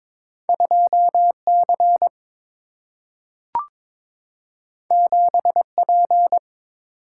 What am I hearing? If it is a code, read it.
Morse code, heard as a keyed tone, about 22 wpm, 700 Hz. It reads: 2C 7P
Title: Single-Letter Single-Number 22wpm